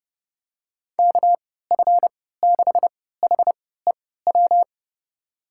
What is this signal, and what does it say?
Morse code, heard as a keyed tone, about 30 wpm, spaced as for 17 wpm, 700 Hz. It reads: KF6HEW
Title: USA Call Signs - 6 Characters - 3x Character Spacing - 30wpm